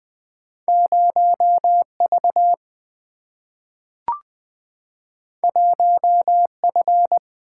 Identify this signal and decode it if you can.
Morse code, heard as a keyed tone, about 20 wpm, 700 Hz. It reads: 0 V 1 F
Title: Single-Letter Single-Number 20wpm